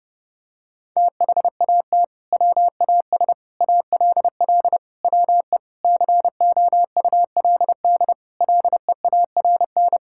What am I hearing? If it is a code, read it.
Morse code, heard as a keyed tone, about 30 wpm, 700 Hz. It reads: THAT WAS ALL WE COULD LEARN
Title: Sentences from Top 1600 Words 30wpm